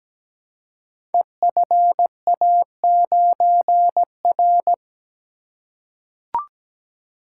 Morse code, heard as a keyed tone, about 17 wpm, 700 Hz. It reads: EFA9R
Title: Code Groups - 5 Letter:Numbers - 17wpm